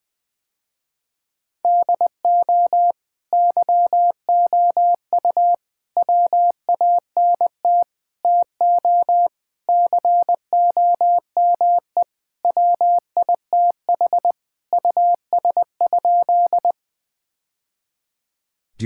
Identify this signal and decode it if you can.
Morse code, heard as a keyed tone, about 20 wpm, 700 Hz. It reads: DO YOU WANT TO COME WITH US?
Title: Sentences with 3-Word Phrases - Spoken Beginning - 20wpm